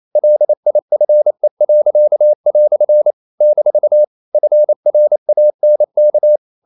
Morse code, heard as a keyed tone, about 28 wpm, 600 Hz. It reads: LIFE." - FRANK
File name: CW / Practice / EnglishQuotations0228